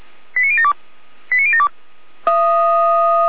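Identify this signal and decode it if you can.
Five-tone selective call (ZVEI), heard as quick runs of five tones, tones of about 80 ms. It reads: 89072 89072
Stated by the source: Five-tone ZVEI selective call received by SDR.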